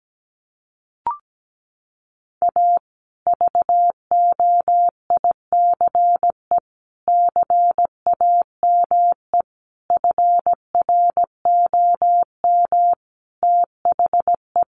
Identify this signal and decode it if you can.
Morse code, heard as a keyed tone, about 17 wpm, 700 Hz. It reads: A VOICE CAME FROM THE
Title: Sentences from Top 1200 Words 17wpm